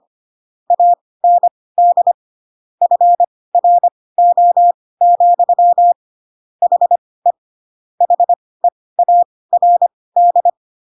Morse code, heard as a keyed tone, about 25 wpm, spaced as for 18 wpm, 700 Hz. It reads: AND FRO, HE HEARD
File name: Pinocchio / Pinocchio ch1 12 25 ik1hge